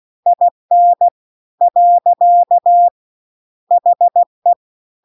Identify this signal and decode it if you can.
Morse code, heard as a keyed tone, about 16 wpm, 700 Hz. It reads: IN . HE